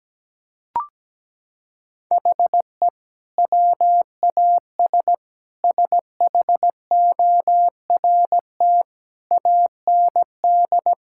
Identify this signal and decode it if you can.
Morse code, heard as a keyed tone, about 17 wpm, 700 Hz. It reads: HE WAS SHORT AND
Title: Sentences from Top 1600 Words 17wpm